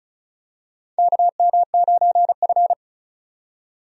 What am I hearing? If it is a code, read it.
Morse code, heard as a keyed tone, about 36 wpm, 700 Hz. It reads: KM9F
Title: USA Call Signs - 4 Characters - 35wpm